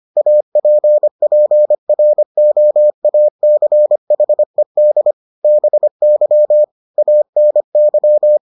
Morse code, heard as a keyed tone, about 25 wpm, 600 Hz. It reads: APPROACHED BY ANY